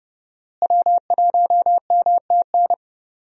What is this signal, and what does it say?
Morse code, heard as a keyed tone, about 29 wpm, 700 Hz. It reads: W1MTN
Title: USA Call Signs - 5 Characters - Repeated 3x - 30wpm